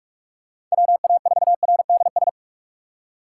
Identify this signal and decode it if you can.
Morse code, heard as a keyed tone, about 46 wpm, 700 Hz. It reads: WA4RDS
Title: USA Call Signs - 3x Character Spacing - 45wpm